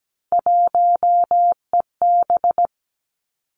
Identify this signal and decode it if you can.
Morse code, heard as a keyed tone, about 17 wpm, 700 Hz. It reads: 1EB
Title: Three Challenging Letter-Number Combinations 17wpm